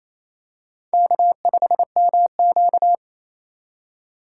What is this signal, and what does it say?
Morse code, heard as a keyed tone, about 28 wpm, 700 Hz. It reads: K5MQ
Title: USA Call Signs - 4 Characters - 8x Character Spacing - 28wpm